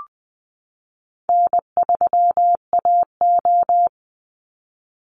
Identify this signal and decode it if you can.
Morse code, heard as a keyed tone, about 20 wpm, 700 Hz. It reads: N3AO
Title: Call Signs - 20wpm